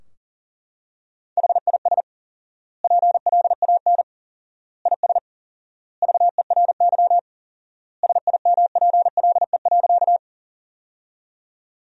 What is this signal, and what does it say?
Morse code, heard as a keyed tone, about 40 wpm, 700 Hz. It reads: HIS PLAN IS VERY SIMPLE.
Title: 4x Word-Spacing - Sentences from Top 500 Words - 40wpm